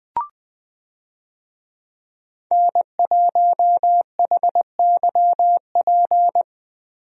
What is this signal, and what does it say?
Morse code, heard as a keyed tone, about 20 wpm, 700 Hz. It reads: N1HYP
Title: USA Call Signs - 5 Characters - Repeated 2x - 20wpm